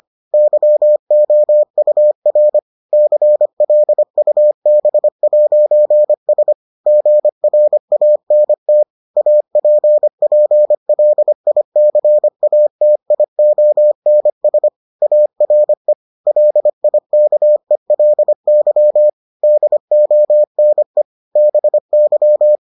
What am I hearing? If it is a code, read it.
Morse code, heard as a keyed tone, about 25 wpm, 600 Hz. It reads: YOUR CLUB'S GRANT APPLICATIONS ARE LIKELY DONE BY